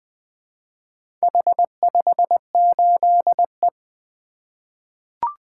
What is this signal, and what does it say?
Morse code, heard as a keyed tone, about 20 wpm, 700 Hz. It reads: H58E
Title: Four Letter-Number Combinations - 20wpm